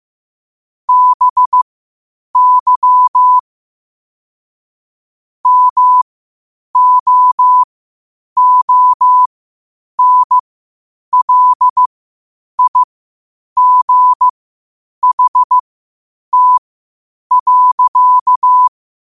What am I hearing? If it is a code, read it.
Morse code, heard as a keyed tone, about 15 wpm, spaced as for 8 wpm, 1000 Hz. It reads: BY MOONLIGHT.
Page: qotd-at-8-wpm